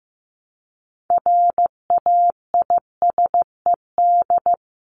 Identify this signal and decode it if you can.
Morse code, heard as a keyed tone, about 15 wpm, 700 Hz. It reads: RAISED
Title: ED Suffix - Top 100 Words with Base Words 15wpm